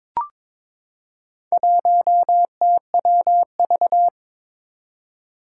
Morse code, heard as a keyed tone, about 22 wpm, 700 Hz. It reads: 1TWV
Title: Four Challenging Letter-Number Combinations 22wpm